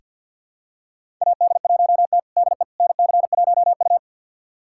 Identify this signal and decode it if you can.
Morse code, heard as a keyed tone, about 47 wpm, 700 Hz. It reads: AD1T DE NC1U
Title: USA Callsign DE Callsign - 4 Characters - 50wpm